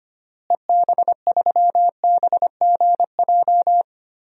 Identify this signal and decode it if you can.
Morse code, heard as a keyed tone, about 25 wpm, 700 Hz. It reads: EB3BGJ
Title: DXCC-100 Call Signs - 6 Characters - Repeated 3x - 25wpm